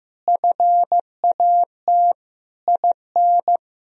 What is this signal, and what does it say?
Morse code, heard as a keyed tone, about 15 wpm, 700 Hz. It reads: FAT IN
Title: Sentences from Top 1000 Words - Review Part 4 - 15wpm